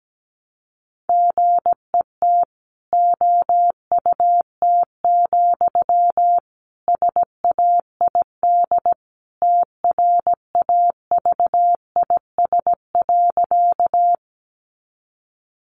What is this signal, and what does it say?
Morse code, heard as a keyed tone, about 17 wpm, 700 Hz. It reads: GET OUT, SAID TRAVIS.
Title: A Sound Of Thunder 17wpm